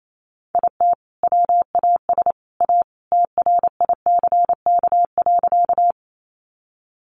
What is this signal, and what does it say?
Morse code, heard as a keyed tone, about 28 wpm, 700 Hz. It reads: IT WAS A TRICK.
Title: Sentences from Top 1500 Words 28wpm